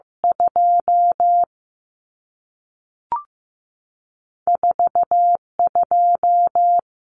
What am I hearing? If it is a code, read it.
Morse code, heard as a keyed tone, about 15 wpm, 700 Hz. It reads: 2 4 2
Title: Numbers - 2-Digit 15wpm